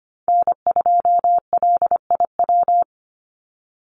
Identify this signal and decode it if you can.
Morse code, heard as a keyed tone, about 25 wpm, 690 Hz. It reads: N2LIW
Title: USA Call Signs 25wpm